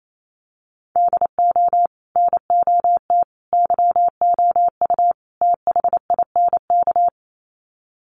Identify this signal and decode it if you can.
Morse code, heard as a keyed tone, about 28 wpm, 700 Hz. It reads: DO NOT YOU THINK
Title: Sets of 4 Words 28wpm